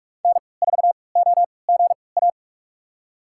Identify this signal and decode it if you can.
Morse code, heard as a keyed tone, about 42 wpm, spaced as for 25 wpm, 700 Hz. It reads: N4OGA
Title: USA Call Signs - 5 Characters - 3x Character Spacing - 45wpm